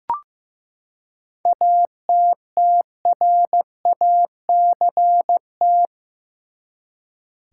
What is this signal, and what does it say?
Morse code, heard as a keyed tone, about 15 wpm, 700 Hz. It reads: ATTRACT
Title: Seven Letter Words - 15wpm